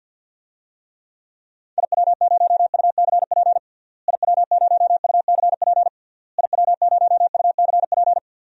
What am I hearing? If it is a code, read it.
Morse code, heard as a keyed tone, about 48 wpm, 700 Hz. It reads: IW0UCP IW0UCP IW0UCP
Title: DXCC-100 Call Signs - 6 Characters - Repeated 3x - 50wpm